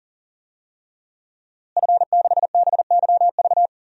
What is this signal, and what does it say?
Morse code, heard as a keyed tone, about 40 wpm, 700 Hz. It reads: F6BYV
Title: DXCC-100 Call Signs - 5 Characters - Repeated 3x - 40wpm